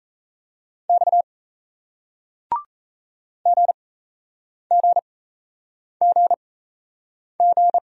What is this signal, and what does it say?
Morse code, heard as a keyed tone, about 38 wpm, spaced as for 8 wpm, 700 Hz. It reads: X GGGG